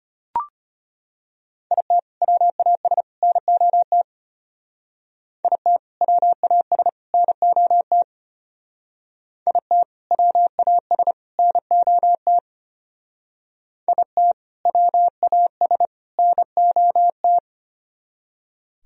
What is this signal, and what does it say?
Morse code, heard as a keyed tone, about 30 wpm, 700 Hz. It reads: IT WAS NOT IT WAS NOT IT WAS NOT IT WAS NOT
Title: Speed-Racing - Sets of 3 Words 25wpm